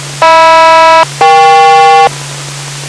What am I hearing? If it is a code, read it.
Aircraft SELCAL, heard as two pairs of tones. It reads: HM-EK